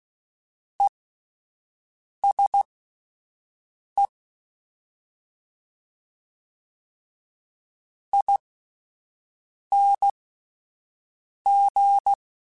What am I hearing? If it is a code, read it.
Morse code, heard as a keyed tone, about 16 wpm, spaced as for 5 wpm, 800 Hz. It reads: ESE ING